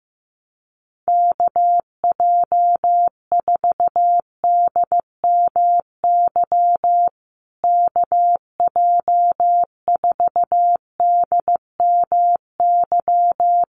Morse code, heard as a keyed tone, about 15 wpm, 700 Hz. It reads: KJ4DMY KJ4DMY
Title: USA Call Signs Repeated 2x 15wpm